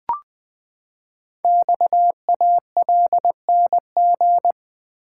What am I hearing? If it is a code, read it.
Morse code, heard as a keyed tone, about 20 wpm, 700 Hz. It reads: XALNG
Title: Code Groups - 5 Letters - 20wpm